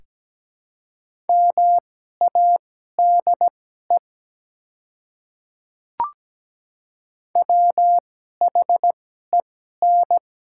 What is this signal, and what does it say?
Morse code, heard as a keyed tone, about 17 wpm, spaced as for 12 wpm, 700 Hz. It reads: MADE WHEN
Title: 2x Character Spacing - Top 200 Words - 17wpm